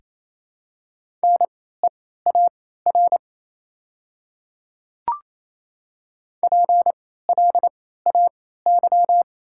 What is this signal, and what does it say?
Morse code, heard as a keyed tone, about 27 wpm, spaced as for 16 wpm, 690 Hz. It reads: NEAR PLAY
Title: 3x Character Spacing - Top 200 Words - 28wpm